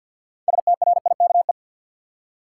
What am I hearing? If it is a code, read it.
Morse code, heard as a keyed tone, about 50 wpm, 700 Hz. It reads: STRIKE
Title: Six Letter Words - 50wpm